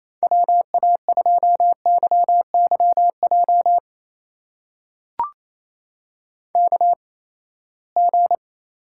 Morse code, heard as a keyed tone, about 28 wpm, spaced as for 19 wpm, 700 Hz. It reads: WA2YYJ KG
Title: USA Call Signs - 6 Characters - 8x Character Spacing - 28wpm